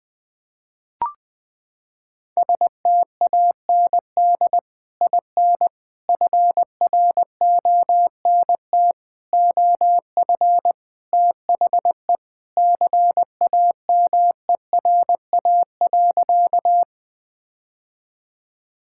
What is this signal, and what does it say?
Morse code, heard as a keyed tone, about 20 wpm, 700 Hz. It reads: STAND IN FRONT OF THE CAMERA.
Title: Sentences with 3-Word Phrases - 20wpm